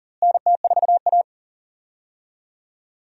Morse code, heard as a keyed tone, about 40 wpm, 700 Hz. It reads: NT4A
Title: USA Call Signs - 4 Characters - Repeated 2x - 40wpm